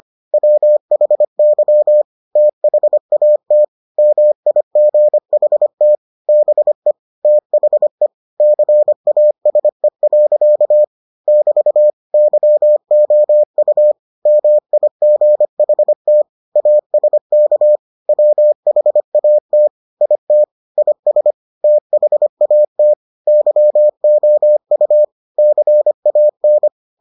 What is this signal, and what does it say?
Morse code, heard as a keyed tone, about 25 wpm, 600 Hz. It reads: WHY THAT MIGHT BE THE CASE. <BT> YOU MIGHT ASK WHAT IT IS THAT YOU CAN